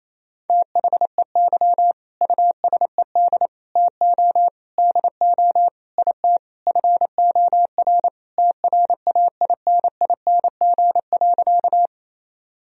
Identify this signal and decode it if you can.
Morse code, heard as a keyed tone, about 28 wpm, 700 Hz. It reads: THEY USED TO DO IT FOR TRAINING.
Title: Sentences from Top 1000 Words - Review Part 4 - 28wpm